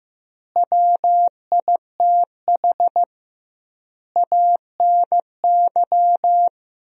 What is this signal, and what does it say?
Morse code, heard as a keyed tone, about 15 wpm, 700 Hz. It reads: WITH ANY
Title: 2x Word-Spacing - Sentences from Top 300 Words - 15wpm